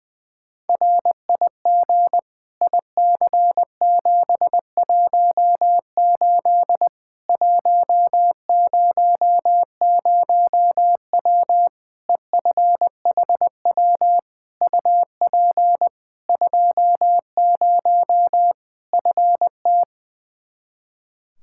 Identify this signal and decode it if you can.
Morse code, heard as a keyed tone, about 20 wpm, 700 Hz. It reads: RIG IC718 100W EFHW UP 20 FT
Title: Practice QSO - Exchange 2 - Info - 20wpm